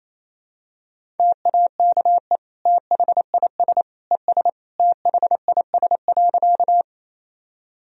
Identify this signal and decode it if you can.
Morse code, heard as a keyed tone, about 28 wpm, 700 Hz. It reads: TAKE THIS ES THIS.
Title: Sentences from Top 200 Words and Abbreviations 28wpm